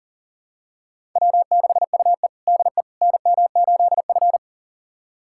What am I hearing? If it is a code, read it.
Morse code, heard as a keyed tone, about 40 wpm, 700 Hz. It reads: W6UE DE NM8F
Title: USA Callsign DE Callsign - 4 Characters - 40wpm